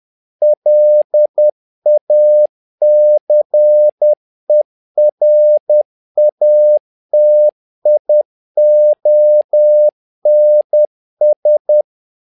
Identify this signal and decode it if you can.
Morse code, heard as a keyed tone, about 10 wpm, 600 Hz. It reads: LACERATIONS
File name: learning center / tower trouble / tower trouble 10wpm